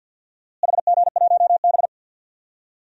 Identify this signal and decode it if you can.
Morse code, heard as a keyed tone, about 50 wpm, 700 Hz. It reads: HG1B
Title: DXCC-100 Call Signs - 4 Characters - Repeated 3x - 50wpm